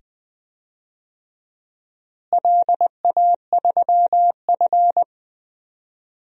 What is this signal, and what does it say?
Morse code, heard as a keyed tone, about 20 wpm, 700 Hz. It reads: LA3F
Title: DXCC-100 Call Signs - 4 Characters - Repeated 2x - 20wpm